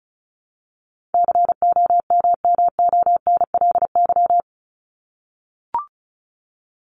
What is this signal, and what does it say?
Morse code, heard as a keyed tone, about 34 wpm, 700 Hz. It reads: COMMONLY
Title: LY Suffix - Top 100 Words with Base Words 35wpm